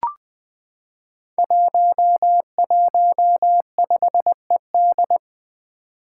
Morse code, heard as a keyed tone, about 20 wpm, 700 Hz. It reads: 115ED
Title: Five Challenging Letter-Number Combinations 20wpm